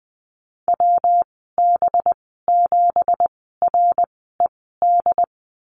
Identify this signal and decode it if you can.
Morse code, heard as a keyed tone, about 20 wpm, spaced as for 14 wpm, 700 Hz. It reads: WB7RED
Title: 186 Lesson 30g - Letter b - Call Signs 20wpm